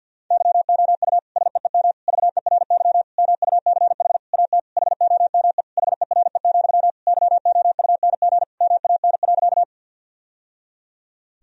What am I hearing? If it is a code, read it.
Morse code, heard as a keyed tone, about 47 wpm, 700 Hz. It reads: YOU SEEM VERY MUCH AT HOME HERE, YOUNG MAN.
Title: Sentences from Top 1000 Words - Review Part 1 - 50wpm